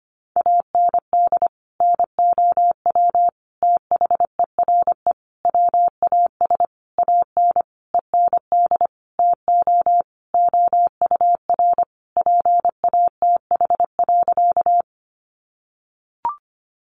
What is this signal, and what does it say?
Morse code, heard as a keyed tone, about 25 wpm, 700 Hz. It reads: AND NOW THERE WAS AN END TO OUR PATH.
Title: Sentences from Top 1400 Words 25wpm